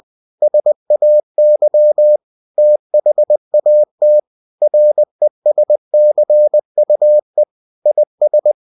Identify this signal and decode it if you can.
Morse code, heard as a keyed tone, about 20 wpm, 600 Hz. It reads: SAY THAT RESCUE IS
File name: cw-books / winnie-the-pooh / winnie-20wpm-page-0098